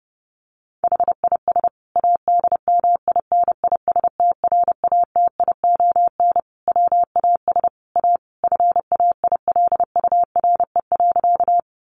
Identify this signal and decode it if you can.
Morse code, heard as a keyed tone, about 30 wpm, 700 Hz. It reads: HIS ADMINISTRATION WAS A FAILURE.
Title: Sentences from Top 1300 Words 30wpm